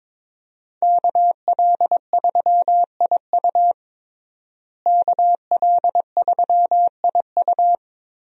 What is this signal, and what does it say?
Morse code, heard as a keyed tone, about 22 wpm, 700 Hz. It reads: KL3IU KL3IU
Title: USA Call Signs - 5 Characters - Repeated 3x - 22wpm